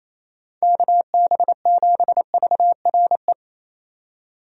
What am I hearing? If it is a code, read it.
Morse code, heard as a keyed tone, about 27 wpm, 700 Hz. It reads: KB7VRE
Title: USA Call Signs - 6 Characters - 28wpm